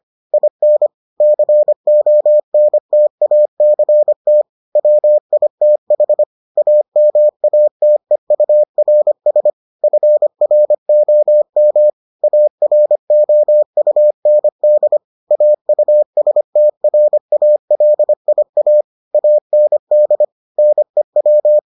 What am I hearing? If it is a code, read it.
Morse code, heard as a keyed tone, about 25 wpm, 600 Hz. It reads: IN CONTACT WITH AMATEURS FROM AROUND AUSTRALIA AND NEW